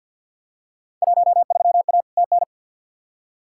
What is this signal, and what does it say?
Morse code, heard as a keyed tone, about 51 wpm, 700 Hz. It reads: 13A TN
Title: Field Day Exchanges Encore - 50wpm